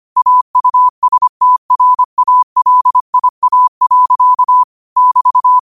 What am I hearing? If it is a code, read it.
Morse code, heard as a keyed tone, about 25 wpm, 1000 Hz. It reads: AUSTRALIA. <BT>
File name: qotdfiles / 25wpm